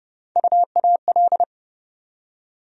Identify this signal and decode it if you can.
Morse code, heard as a keyed tone, about 30 wpm, 700 Hz. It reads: UAL
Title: USA Call Sign Postfixes - 3 Letters 30wpm